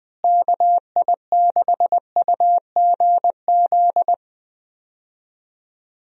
Morse code, heard as a keyed tone, about 20 wpm, 700 Hz. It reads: KI6UGZ
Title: USA Call Signs - 6 Characters - Repeated 2x - 20wpm